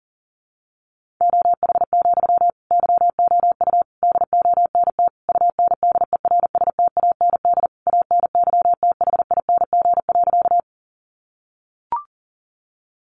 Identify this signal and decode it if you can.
Morse code, heard as a keyed tone, about 40 wpm, 700 Hz. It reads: OH, YOU DONT UNDERSTAND ANYTHING.
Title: Sentences from Top 1000 Words 40wpm